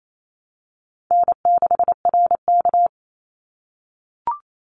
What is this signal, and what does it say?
Morse code, heard as a keyed tone, about 28 wpm, 700 Hz. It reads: N6RK
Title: Call Signs - 28wpm